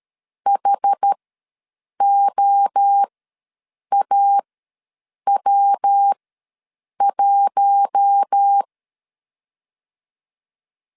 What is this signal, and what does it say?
Morse code, heard as a keyed tone, about 13 wpm, spaced as for 7 wpm, 800 Hz. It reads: HOAW1
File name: LEZIONI / 6 PKX0 / 6 P ws 12wpm 3117